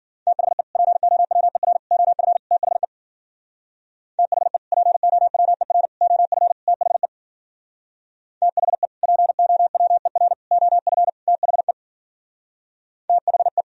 Morse code, heard as a keyed tone, about 52 wpm, 700 Hz. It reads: THE POWER OF THE THE POWER OF THE THE POWER OF THE THE